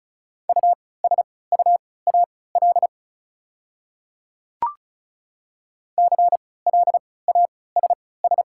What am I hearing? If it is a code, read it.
Morse code, heard as a keyed tone, about 36 wpm, spaced as for 20 wpm, 700 Hz. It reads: USUAL CLASS
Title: 3x Character Spacing - Top 500 Words - 35wpm